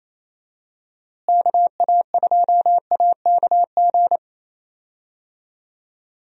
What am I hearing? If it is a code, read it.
Morse code, heard as a keyed tone, about 28 wpm, 700 Hz. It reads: KA2AKG
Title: USA Call Signs - 6 Characters - Repeated 3x - 28wpm